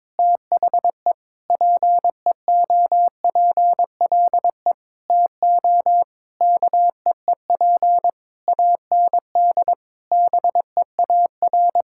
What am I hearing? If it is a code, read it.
Morse code, heard as a keyed tone, about 22 wpm, 700 Hz. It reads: THE PEOPLE TO KEEP AND BEAR
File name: Sets of 9 Words - 22wpm